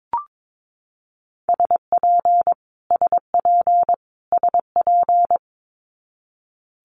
Morse code, heard as a keyed tone, about 22 wpm, 700 Hz. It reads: SP SP SP
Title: Multi-letter Phonemes Repeated 3x 22wpm